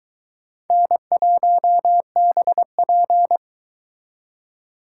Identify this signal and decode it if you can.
Morse code, heard as a keyed tone, about 22 wpm, 700 Hz. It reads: N1BP